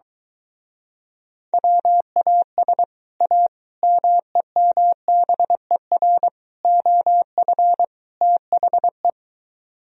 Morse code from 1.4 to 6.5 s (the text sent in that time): WAS A MEMBER